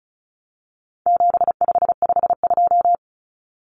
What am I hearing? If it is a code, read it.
Morse code, heard as a keyed tone, about 34 wpm, 700 Hz. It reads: 7552
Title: Common Cut Numbers - 4-Digit 35wpm